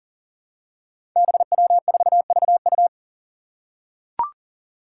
Morse code, heard as a keyed tone, about 43 wpm, 700 Hz. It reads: BW4VU
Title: Five Challenging Letter-Number Combinations 40wpm